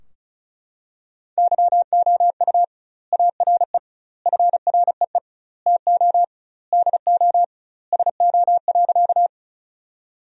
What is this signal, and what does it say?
Morse code, heard as a keyed tone, about 34 wpm, 700 Hz. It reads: YOU ARE FREE TO DO SO.
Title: 2x Word-Spacing - Sentences from Top 500 Words - 35wpm